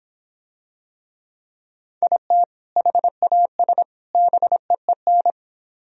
Morse code, heard as a keyed tone, about 26 wpm, 700 Hz. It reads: IT HAS BEEN